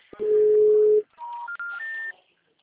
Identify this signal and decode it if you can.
Special information tone.
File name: out- -2013- - - .86735